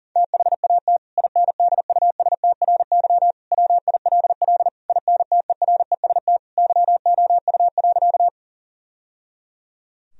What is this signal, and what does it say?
Morse code, heard as a keyed tone, about 40 wpm, 700 Hz. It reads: THAT INDUSTRY WILL INTEREST YOU.
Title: Sentences from Top 1000 Words - Review Part 4 - 40wpm